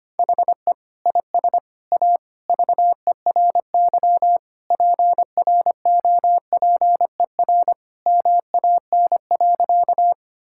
Morse code, heard as a keyed tone, about 25 wpm, 700 Hz. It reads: HE IS A VERY PROPER MAN.
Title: Sentences from Top 1000 Words - Review Part 5 - 25wpm